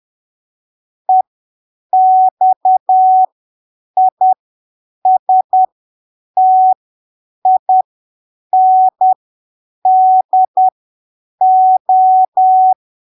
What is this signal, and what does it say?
Morse code, heard as a keyed tone, about 10 wpm, spaced as for 7 wpm, 750 Hz. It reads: EXISTINDO
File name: arquivos / texto / texto 04 10ppm